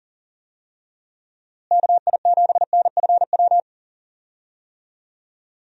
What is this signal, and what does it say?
Morse code, heard as a keyed tone, about 40 wpm, 700 Hz. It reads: KI7NFW
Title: USA Call Signs - 6 Characters - Repeated 2x - 40wpm